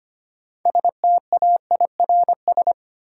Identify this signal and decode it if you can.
Morse code, heard as a keyed tone, about 25 wpm, 700 Hz. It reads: STAIRS
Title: Sets of 5 Words Encore - New Word Drill - 25wpm